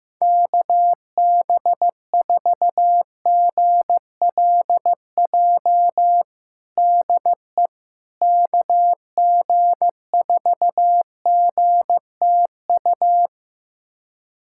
Morse code, heard as a keyed tone, about 15 wpm, 700 Hz. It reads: KB4GLJ DE KG4GTU
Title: USA Callsign DE Callsign - 6 Characters - 15wpm